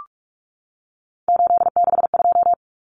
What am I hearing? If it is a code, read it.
Morse code, heard as a keyed tone, about 44 wpm, 700 Hz. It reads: 862
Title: Numbers - 3-Digit 45wpm